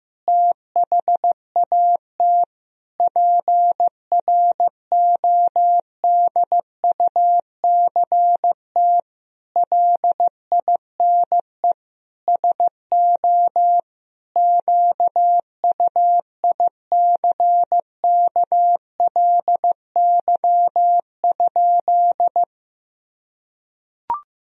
Morse code, heard as a keyed tone, about 15 wpm, 700 Hz. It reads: THAT PRODUCT LINE SO QUICKLY?
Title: Sentences from Top 1000 Words - Review Part 3 - 15wpm